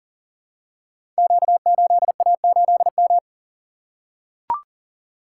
Morse code, heard as a keyed tone, about 39 wpm, 700 Hz. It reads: Q8A8M
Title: Code Groups - 5 Letter:Numbers - 40wpm